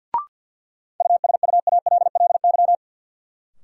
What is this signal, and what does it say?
Morse code, heard as a keyed tone, about 53 wpm, 700 Hz. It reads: USUALLY
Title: Sentences from Top 1000 Words - Review Part 3 - New Word Drill - 50wpm